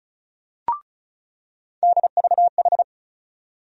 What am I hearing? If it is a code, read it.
Morse code, heard as a keyed tone, about 36 wpm, 700 Hz. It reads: DVH
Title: USA Call Sign Postfixes - 3 Letters 35wpm